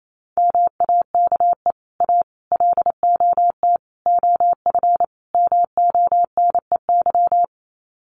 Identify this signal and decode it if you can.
Morse code, heard as a keyed tone, about 28 wpm, 700 Hz. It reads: MAKE A LOT OF MONEY
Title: Sets of 5 Words Encore2 28wpm